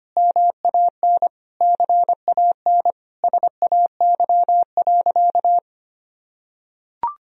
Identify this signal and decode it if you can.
Morse code, heard as a keyed tone, about 25 wpm, 700 Hz. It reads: MAN CAN SAY.
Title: Sentences from Top 1000 Words - Review Part 2 - 25wpm